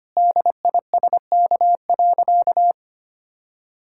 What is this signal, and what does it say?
Morse code, heard as a keyed tone, about 25 wpm, 700 Hz. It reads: DISK.
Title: Sentences from Top 1000 Words - Review Part 6 - 25wpm